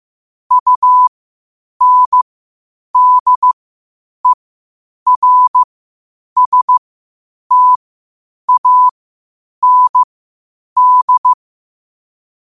Morse code, heard as a keyed tone, about 15 wpm, spaced as for 8 wpm, 1000 Hz. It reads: UNDERSTAND